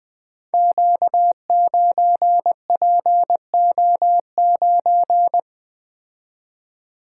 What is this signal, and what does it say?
Morse code, heard as a keyed tone, about 21 wpm, 700 Hz. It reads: Q9PO9
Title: Code Groups - 5 Letter:Numbers - 20wpm